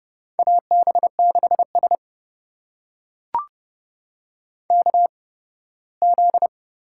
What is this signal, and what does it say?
Morse code, heard as a keyed tone, about 30 wpm, spaced as for 18 wpm, 700 Hz. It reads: AB6S KZ
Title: USA Call Signs - 4 Characters - 8x Character Spacing - 30wpm